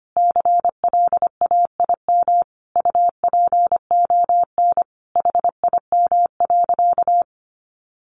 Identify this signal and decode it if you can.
Morse code, heard as a keyed tone, about 25 wpm, 700 Hz. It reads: CLAIM UPON HIM.
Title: Sentences from Top 1500 Words 25wpm